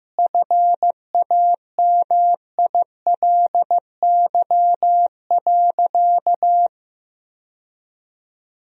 Morse code, heard as a keyed tone, about 15 wpm, 700 Hz. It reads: FAMILY.
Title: Sentences from Top 1000 Words - Review Part 4 - 15wpm